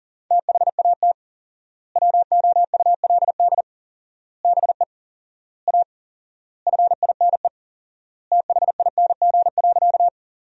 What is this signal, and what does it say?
Morse code, heard as a keyed tone, about 40 wpm, 700 Hz. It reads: THAT WOULD BE A FINE THING.
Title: 4x Word-Spacing - Sentences from Top 500 Words - 40wpm